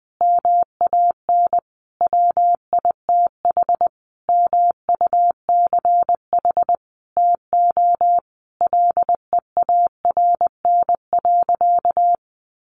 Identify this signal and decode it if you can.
Morse code, heard as a keyed tone, about 20 wpm, 700 Hz. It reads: MAN WITH MUCH TO LEARN.
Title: Sentences from Top 1400 Words 20wpm